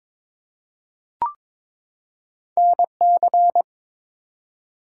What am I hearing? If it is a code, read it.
Morse code, heard as a keyed tone, about 22 wpm, 700 Hz. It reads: NC
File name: ARRL Sections v2 - 22wpm